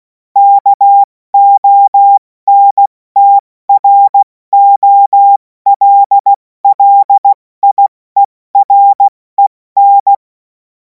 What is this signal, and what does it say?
Morse code, heard as a keyed tone, about 16 wpm, spaced as for 14 wpm, 800 Hz. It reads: KONTROLLIEREN
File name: Morsekurs / Witze / Witz 03. Februar 2026 70 F80